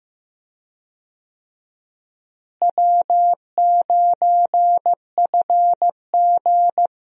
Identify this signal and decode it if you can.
Morse code, heard as a keyed tone, about 15 wpm, 700 Hz. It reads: W9FG
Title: USA Call Signs - 4 Characters - Repeated 2x - 15wpm